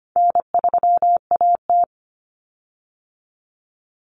Morse code, heard as a keyed tone, about 25 wpm, 700 Hz. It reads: N3AT
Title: Common Cut Numbers - 4-Digit 25wpm